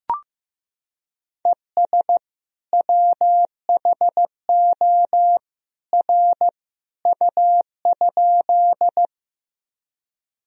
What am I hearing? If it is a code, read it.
Morse code, heard as a keyed tone, about 15 wpm, 700 Hz. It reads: ES WHO R U?
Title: Sentences from Top 100 Words and Abbreviations 15wpm